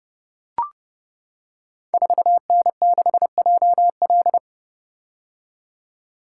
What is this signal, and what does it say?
Morse code, heard as a keyed tone, about 30 wpm, 700 Hz. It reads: 4N6JL
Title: Code Groups - 5 Letter:Numbers - 30wpm